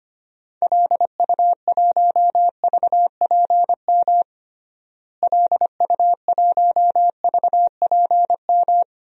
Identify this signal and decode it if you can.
Morse code, heard as a keyed tone, about 25 wpm, 700 Hz. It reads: LU1VPM LU1VPM
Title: DXCC-100 Call Signs - 6 Characters - Repeated 2x - 25wpm